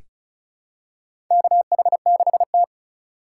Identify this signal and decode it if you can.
Morse code, heard as a keyed tone, about 36 wpm, 700 Hz. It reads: KH6T
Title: USA Call Signs - 4 Characters - 4x Character Spacing - 35wpm